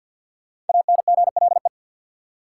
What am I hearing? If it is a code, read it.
Morse code, heard as a keyed tone, about 48 wpm, 700 Hz. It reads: ANGLE
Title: Sentences from Top 1500 Words - New Word Drill - 50wpm